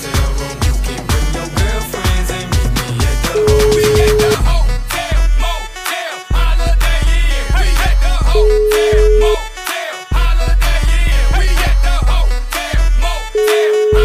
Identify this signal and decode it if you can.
Ringback tone (Europe, 425 Hz), with noise.